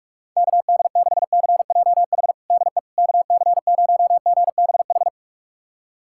Morse code, heard as a keyed tone, about 46 wpm, 700 Hz. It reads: KD6CJH DE KC0GBH
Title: USA Callsign DE Callsign 45wpm